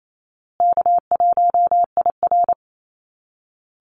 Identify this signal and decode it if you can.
Morse code, heard as a keyed tone, about 28 wpm, 700 Hz. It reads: K1IR
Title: Easy Call Signs Repeated 3x - 28wpm